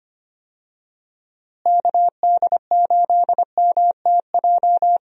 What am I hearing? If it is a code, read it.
Morse code, heard as a keyed tone, about 25 wpm, 700 Hz. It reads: KD8MTJ
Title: USA Call Signs - 6 Characters - Repeated 3x - 25wpm